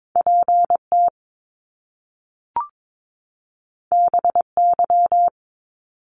Morse code, heard as a keyed tone, about 22 wpm, 700 Hz. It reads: PT BY
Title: Top Two Letter Combinations in Words 22wpm